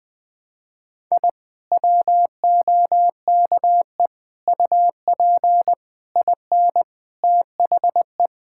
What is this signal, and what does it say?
Morse code, heard as a keyed tone, about 19 wpm, 700 Hz. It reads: I WOKE UP IN THE
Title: Sentences with 5-Word Phrases - Spoken Beginning - 20wpm